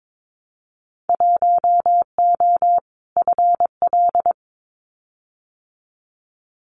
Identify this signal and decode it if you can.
Morse code, heard as a keyed tone, about 22 wpm, 700 Hz. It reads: 1O FL
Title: Winter Field Day Exchanges 22wpm